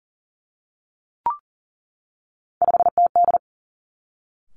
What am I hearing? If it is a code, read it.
Morse code, heard as a keyed tone, about 40 wpm, 700 Hz. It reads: 5TD
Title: Three Challenging Letter-Number Combinations 40wpm